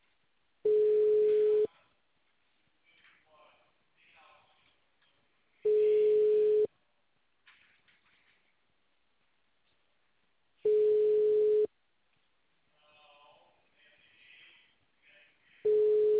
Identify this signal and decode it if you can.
Ringback tone (Europe, 425 Hz), with speech.